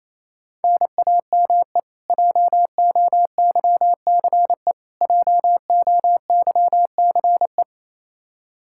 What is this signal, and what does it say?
Morse code, heard as a keyed tone, about 28 wpm, 700 Hz. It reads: NAME JOYCE JOYCE
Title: Practice QSO - Exchange 1 - Beginning - 28wpm